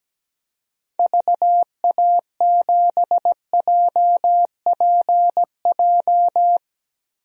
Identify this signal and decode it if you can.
Morse code, heard as a keyed tone, about 17 wpm, 700 Hz. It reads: VA7JPJ
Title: DXCC-100 Call Signs - 6 Characters - 17wpm